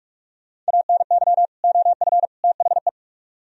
Morse code, heard as a keyed tone, about 58 wpm, 700 Hz. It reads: ANY OF THE